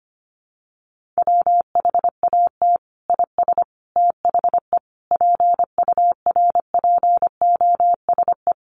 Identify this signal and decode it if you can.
Morse code, heard as a keyed tone, about 25 wpm, 700 Hz. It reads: WHAT IS THE PURPOSE
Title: Sentences from Top 1400 Words 25wpm